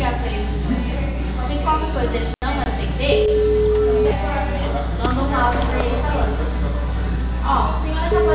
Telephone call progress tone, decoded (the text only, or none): ringback tone